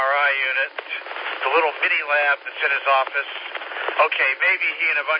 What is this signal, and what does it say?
Emergency Alert System (SAME) tones (header bursts), heard as harsh buzzing bursts, with speech. It is heard from 0.3 to 4.6 s.